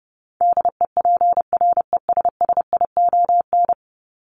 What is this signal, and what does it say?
Morse code, heard as a keyed tone, about 28 wpm, 700 Hz. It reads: DEPRESSION
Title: Top 1001-1250 Words 30wpm